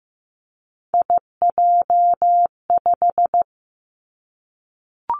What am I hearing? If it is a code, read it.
Morse code, heard as a keyed tone, about 15 wpm, 700 Hz. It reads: IJ5
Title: Three Challenging Letter-Number Combinations 15wpm